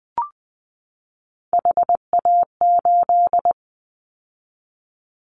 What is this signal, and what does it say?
Morse code, heard as a keyed tone, about 20 wpm, 700 Hz. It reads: HA8
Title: Callsign Prefixes for Top 25 DXCC - 20wpm